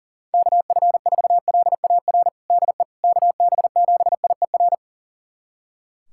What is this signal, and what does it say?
Morse code, heard as a keyed tone, about 40 wpm, 700 Hz. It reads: KF4LAR DE KB7IER
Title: USA Callsign DE Callsign - 6 Characters - 40wpm